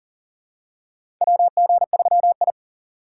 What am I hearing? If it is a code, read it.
Morse code, heard as a keyed tone, about 37 wpm, 700 Hz. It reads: WG3I
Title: USA Call Signs - 4 Characters - 4x Character Spacing - 40wpm